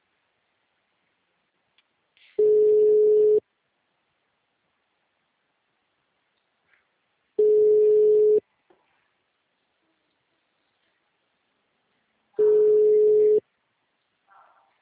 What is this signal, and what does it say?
Ringback tone (Europe, 425 Hz).